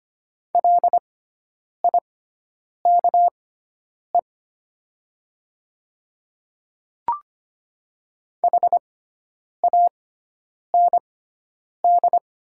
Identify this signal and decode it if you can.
Morse code, heard as a keyed tone, about 25 wpm, spaced as for 9 wpm, 700 Hz. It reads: LIKE HAND
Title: 6x Character Spacing - Top 500 Words - 25wpm